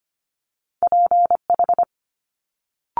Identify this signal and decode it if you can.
Morse code, heard as a keyed tone, about 25 wpm, 700 Hz. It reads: PH
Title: Multi-letter Phonemes Quick Fire 25wpm